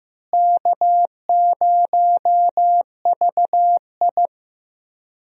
Morse code, heard as a keyed tone, about 15 wpm, 700 Hz. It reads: K0VI
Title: USA Call Signs - 4 Characters - Repeated 3x - 15wpm